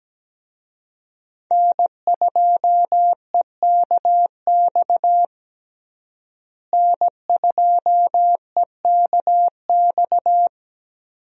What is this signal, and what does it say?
Morse code, heard as a keyed tone, about 17 wpm, 700 Hz. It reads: N2EKX N2EKX
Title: USA Call Signs - 5 Characters - Repeated 2x - 17wpm